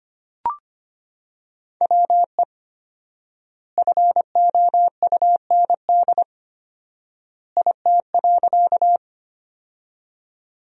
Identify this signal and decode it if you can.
Morse code, heard as a keyed tone, about 25 wpm, 700 Hz. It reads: WE FOUND IT.
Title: 4x Word-Spacing - Sentences from Top 300 Words - 25wpm